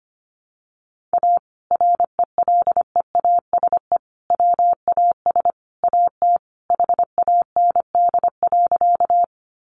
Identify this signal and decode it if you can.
Morse code, heard as a keyed tone, about 25 wpm, 690 Hz. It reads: A RELEASE WAS AT HAND.
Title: Sentences from Top 1100 Words 25wpm